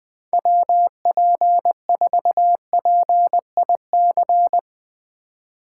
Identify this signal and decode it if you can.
Morse code, heard as a keyed tone, about 21 wpm, 700 Hz. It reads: WP4PIC
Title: Speed Racing - USA 2x3 Call Signs 20wpm